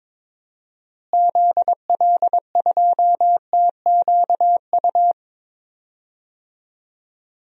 Morse code, heard as a keyed tone, about 22 wpm, 700 Hz. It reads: ZL2TQU
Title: DXCC-100 Call Signs - 6 Characters - Repeated 2x - 22wpm